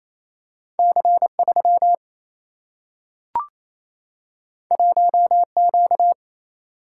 Morse code, heard as a keyed tone, about 28 wpm, 700 Hz. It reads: C 3 1 Q
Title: Single-Letter Single-Number 28wpm